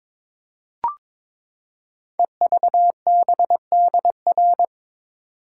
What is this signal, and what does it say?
Morse code, heard as a keyed tone, about 22 wpm, 700 Hz. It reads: EVBDR
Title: Code Groups - 5 Letter:Numbers - 22wpm